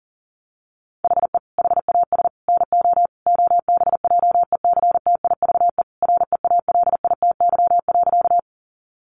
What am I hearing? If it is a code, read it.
Morse code, heard as a keyed tone, about 40 wpm, 700 Hz. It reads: HE HAS NO OBJECTIVE REALITY.
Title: Sentences from Top 1200 Words 40wpm